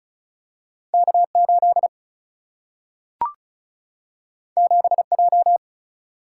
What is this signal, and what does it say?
Morse code, heard as a keyed tone, about 36 wpm, 700 Hz. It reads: K8 7J
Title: Single-Letter Single-Number 35wpm